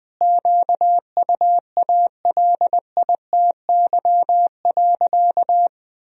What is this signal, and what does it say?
Morse code, heard as a keyed tone, about 20 wpm, 700 Hz. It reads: QUALITY.
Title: Sentences from Top 1000 Words - Review Part 4 - 20wpm